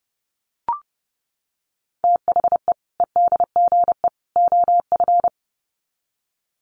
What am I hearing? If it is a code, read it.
Morse code, heard as a keyed tone, about 30 wpm, 700 Hz. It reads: THE EDGE OF
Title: Sets of 3 Words 30wpm